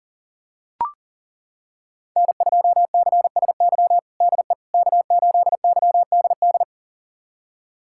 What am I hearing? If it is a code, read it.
Morse code, heard as a keyed tone, about 40 wpm, 700 Hz. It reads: N2CSY DE K8YDD
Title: USA Callsign DE Callsign - 5 Characters - 40wpm